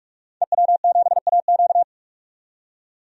Morse code, heard as a keyed tone, about 42 wpm, 700 Hz. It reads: EW7AQ
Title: DXCC-100 Call Signs - 5 Characters - 45wpm